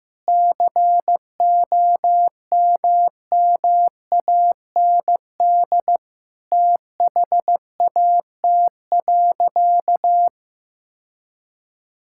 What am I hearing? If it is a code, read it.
Morse code, heard as a keyed tone, about 16 wpm, 700 Hz. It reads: COMMAND THAT.
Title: Sentences from Top 1600 Words 15wpm